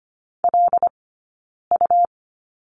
Morse code, heard as a keyed tone, about 25 wpm, 700 Hz. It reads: L U
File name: Lesson 17d - Letter u - Two Characters 25wpm